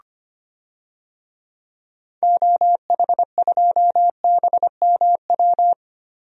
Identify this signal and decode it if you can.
Morse code, heard as a keyed tone, about 25 wpm, 700 Hz. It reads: OH2BMW
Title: DXCC-100 Call Signs - 6 Characters - Repeated 2x - 25wpm